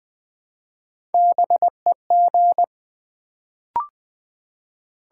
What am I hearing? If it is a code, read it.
Morse code, heard as a keyed tone, about 20 wpm, 690 Hz. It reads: BEG
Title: Build A Word - 1 Letter At A Time - 3-Letter Words - 20wpm